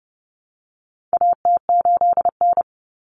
Morse code, heard as a keyed tone, about 30 wpm, 700 Hz. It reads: AT8N
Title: Common Cut Numbers - 4-Digit 30wpm